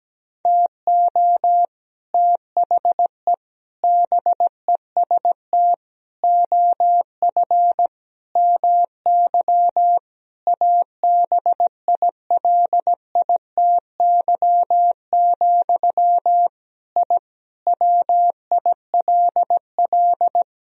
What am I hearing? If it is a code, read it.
Morse code, heard as a keyed tone, about 17 wpm, 700 Hz. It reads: TO THE BEST OF MY ABILITY, I WILL
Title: Sentences from Top 1000 Words - Review Part 3 - 17wpm